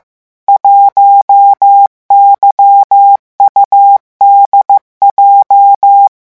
Morse code, heard as a keyed tone, about 15 wpm, 800 Hz. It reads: 1YUDJ
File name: morse mp3 / revision12-vitesse 14 mots